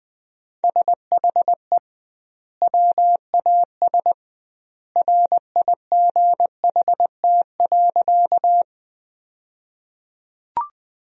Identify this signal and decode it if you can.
Morse code, heard as a keyed tone, about 20 wpm, 700 Hz. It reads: SHE WAS RIGHT.
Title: 2x Word-Spacing - Sentences from Top 200 Words - 20wpm